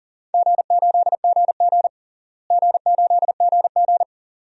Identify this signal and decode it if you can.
Morse code, heard as a keyed tone, about 40 wpm, 700 Hz. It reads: G8GG G8GG
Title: DXCC-100 Call Signs - 4 Characters - Repeated 2x - 40wpm